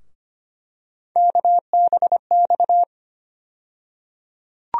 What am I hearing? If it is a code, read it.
Morse code, heard as a keyed tone, about 25 wpm, 700 Hz. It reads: KBX
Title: USA Call Sign Postfixes - 3 Letters 25wpm